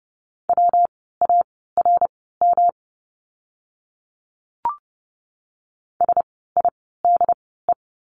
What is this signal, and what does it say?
Morse code, heard as a keyed tone, about 30 wpm, spaced as for 17 wpm, 700 Hz. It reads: WARM SIDE
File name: course / 30:10wpm / 115 Lesson 19f - Letter w - Words 30wpm